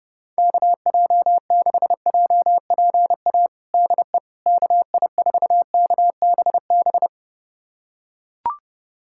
Morse code, heard as a keyed tone, about 30 wpm, 700 Hz. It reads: KJ6JPA DE KI4KBB
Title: USA Callsign DE Callsign - 6 Characters - 30wpm